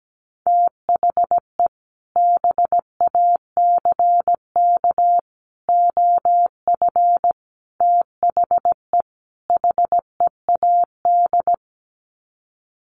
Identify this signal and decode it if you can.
Morse code, heard as a keyed tone, about 17 wpm, 700 Hz. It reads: THE BACK OF THE HEAD
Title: Sets of 5 Words Encore2 17wpm